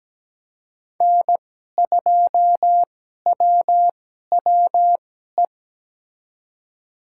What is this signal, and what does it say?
Morse code, heard as a keyed tone, about 17 wpm, spaced as for 12 wpm, 700 Hz. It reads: N2WWE
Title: USA Call Signs - 2x Character Spacing - 17wpm